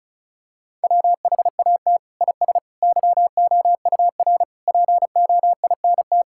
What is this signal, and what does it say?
Morse code, heard as a keyed tone, about 34 wpm, 700 Hz. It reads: WHAT IS YOUR POINT
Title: Sentences from Top 1000 Words - Review Part 6 - 35wpm